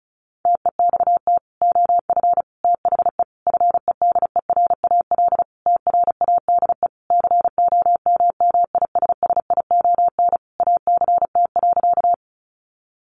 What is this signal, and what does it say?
Morse code, heard as a keyed tone, about 34 wpm, 700 Hz. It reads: TEXT OF THE FEDERAL TRADE COMMISSION ACT.
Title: Sentences from Top 1400 Words 35wpm